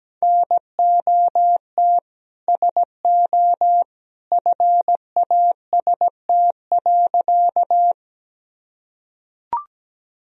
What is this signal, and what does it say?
Morse code, heard as a keyed tone, about 17 wpm, 700 Hz. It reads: NOT SO FAST.
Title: Sentences from Top 1000 Words - Review Part 2 - 17wpm